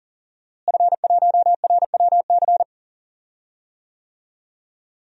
Morse code, heard as a keyed tone, about 40 wpm, 700 Hz. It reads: F1RWC
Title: DXCC-100 Call Signs - 5 Characters - Repeated 3x - 40wpm